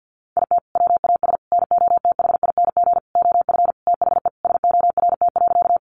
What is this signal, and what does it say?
Morse code, heard as a keyed tone, about 50 wpm, 700 Hz. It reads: IT WAS NOTHING OF THE SORT.
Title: Sentences from Top 1300 Words 50wpm